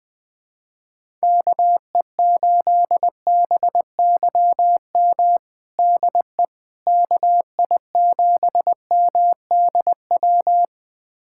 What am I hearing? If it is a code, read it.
Morse code, heard as a keyed tone, about 20 wpm, 700 Hz. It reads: KE8BYM DE KI7MDW
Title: USA Callsign DE Callsign - 6 Characters - 20wpm